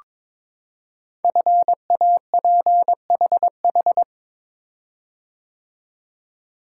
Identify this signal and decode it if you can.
Morse code, heard as a keyed tone, about 22 wpm, 700 Hz. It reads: FAPHH
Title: Code Groups - 5 Letters - 22wpm